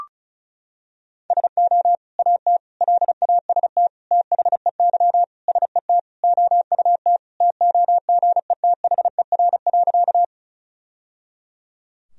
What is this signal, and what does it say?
Morse code, heard as a keyed tone, about 36 wpm, 700 Hz. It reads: SO AT LAST THEY SET OUT TOGETHER.
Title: Sentences from Top 1000 Words - Review Part 1 - 35wpm